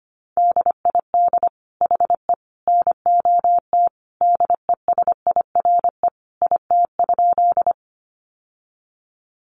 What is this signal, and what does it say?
Morse code, heard as a keyed tone, about 25 wpm, 700 Hz. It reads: DID HE NOT DESIRE IT?
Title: Sentences from Top 1100 Words 25wpm